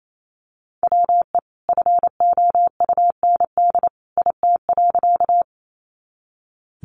Morse code, heard as a keyed tone, about 28 wpm, 700 Hz. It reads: WE FOUND IT.
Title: Sentences from Top 300 Words 28wpm